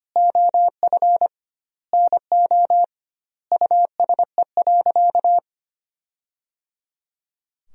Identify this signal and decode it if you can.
Morse code, heard as a keyed tone, about 25 wpm, 700 Hz. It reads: OF NO USE.
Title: 2x Word-Spacing - Sentences from Top 100 Words - 25wpm